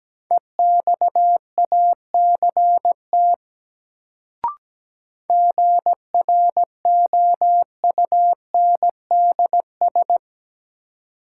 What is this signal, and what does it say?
Morse code, heard as a keyed tone, about 17 wpm, 700 Hz. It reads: EXACT GROUNDS
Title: Sentences from Top 1300 Words - New Word Drill - 17wpm